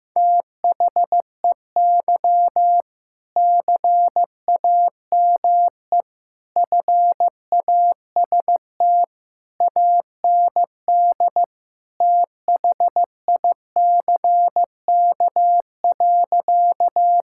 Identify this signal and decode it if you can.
Morse code, heard as a keyed tone, about 15 wpm, 700 Hz. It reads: THEY CAME FAST AND THICK.
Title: Sentences from Top 1600 Words 15wpm